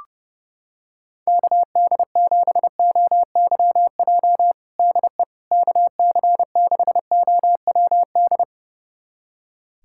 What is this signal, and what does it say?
Morse code, heard as a keyed tone, about 30 wpm, 700 Hz. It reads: KD7OYJ DE KC6OWD
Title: USA Callsign DE Callsign - 6 Characters - 30wpm